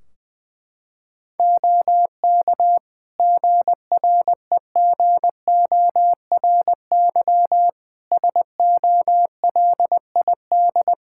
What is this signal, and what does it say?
Morse code, heard as a keyed tone, about 20 wpm, 700 Hz. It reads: OK GREGORY SOLID
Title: Practice QSO - Exchange 3 - More Info - 20wpm